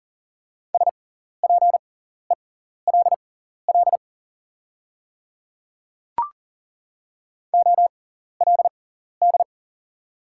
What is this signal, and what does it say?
Morse code, heard as a keyed tone, about 41 wpm, spaced as for 14 wpm, 700 Hz. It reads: SPELL OLD
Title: 6x Character Spacing - Top 500 Words - 40wpm